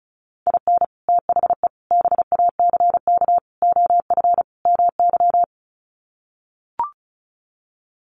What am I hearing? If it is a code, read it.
Morse code, heard as a keyed tone, about 34 wpm, 700 Hz. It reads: IN THE BACK OF MY
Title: Sets of 5 Words Encore 35wpm